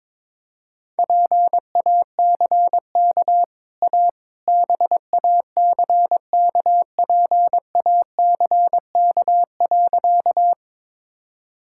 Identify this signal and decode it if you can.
Morse code, heard as a keyed tone, about 22 wpm, 690 Hz. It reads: PACK A BACKPACK.
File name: practice / 22wpm / Easy Sentences with Hints - 22wpm